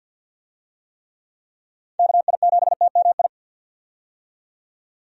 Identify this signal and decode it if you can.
Morse code, heard as a keyed tone, about 51 wpm, 700 Hz. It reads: KI7TMI
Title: USA Call Signs - 6 Characters - 50wpm